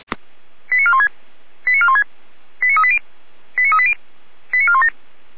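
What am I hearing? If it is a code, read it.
Five-tone selective call (ZVEI), heard as quick runs of five tones, tones of about 70 ms. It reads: 89416 89416 89380 89380 89417
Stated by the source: ZVEI five-tone call received by SDR, from the Feuerwehr Orlamünde (fire brigade) audio server.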